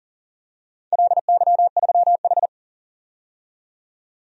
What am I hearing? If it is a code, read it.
Morse code, heard as a keyed tone, about 40 wpm, 700 Hz. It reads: LY3H
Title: DXCC-100 Call Signs - 4 Characters - 40wpm